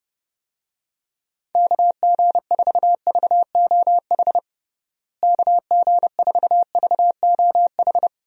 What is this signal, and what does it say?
Morse code, heard as a keyed tone, about 30 wpm, 700 Hz. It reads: KG4VOH KG4VOH
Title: USA Call Signs - 6 Characters - Repeated 3x - 30wpm